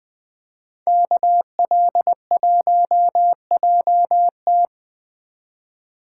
Morse code, heard as a keyed tone, about 20 wpm, 700 Hz. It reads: KL1JT
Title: Speed Racing - USA 2x2 Call Signs 17wpm